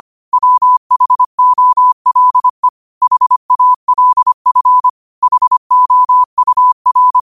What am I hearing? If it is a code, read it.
Morse code, heard as a keyed tone, about 25 wpm, 1000 Hz. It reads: WHOLE HALF HOUR